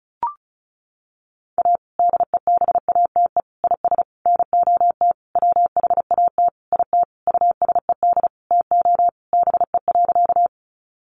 Morse code, heard as a keyed tone, about 34 wpm, 700 Hz. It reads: A DEBATE IS NOT WHAT IT USED TO BE.
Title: Sentences from Top 1100 Words 35wpm